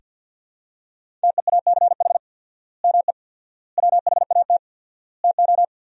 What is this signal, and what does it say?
Morse code, heard as a keyed tone, about 47 wpm, 700 Hz. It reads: TEACH ME WHAT TO
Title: 4x Word-Spacing - Sentences from Top 500 Words - 50wpm